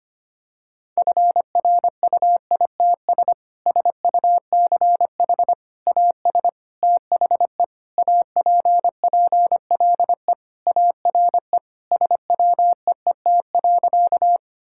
Morse code, heard as a keyed tone, about 25 wpm, 700 Hz. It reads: FRUITS SUCH AS THE APPLE ARE SWEET.
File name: practice / 25wpm / Sentences with 3-Word Phrases - 25wpm